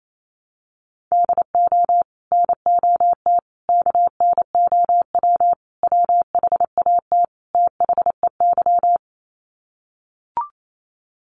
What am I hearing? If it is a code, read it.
Morse code, heard as a keyed tone, about 28 wpm, 700 Hz. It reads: DO NOT KNOW WHAT THEY
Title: Sets of 5 Words 28wpm